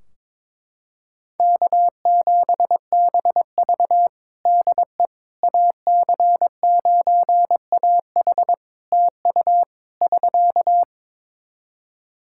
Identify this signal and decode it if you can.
Morse code, heard as a keyed tone, about 22 wpm, 700 Hz. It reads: K7BV DE AC9AH TU <SK>
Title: Practice QSO - Exchange 4 - Ending - 22wpm